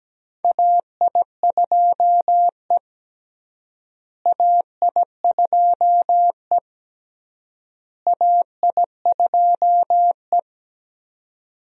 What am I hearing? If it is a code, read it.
Morse code, heard as a keyed tone, about 17 wpm, 700 Hz. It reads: AI2E AI2E AI2E
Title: USA Call Signs - 4 Characters - Repeated 3x - 17wpm